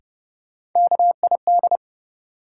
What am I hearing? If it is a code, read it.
Morse code, heard as a keyed tone, about 30 wpm, 700 Hz. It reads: KID
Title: Sets of 5 Words - New Word Drill - 30wpm